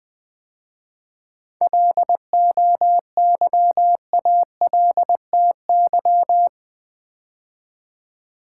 Morse code, heard as a keyed tone, about 20 wpm, 700 Hz. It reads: LOYALTY
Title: 7 Letter Words - First 3 Letters Given - 20wpm